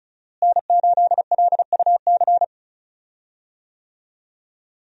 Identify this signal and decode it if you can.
Morse code, heard as a keyed tone, about 36 wpm, 700 Hz. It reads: N8LUC
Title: USA Call Signs - 5 Characters - Repeated 2x - 35wpm